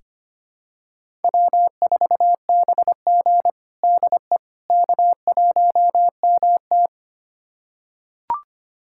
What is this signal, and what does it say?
Morse code, heard as a keyed tone, about 25 wpm, 700 Hz. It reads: W4BG DE K1MT
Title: USA Callsign DE Callsign - 4 Characters - 25wpm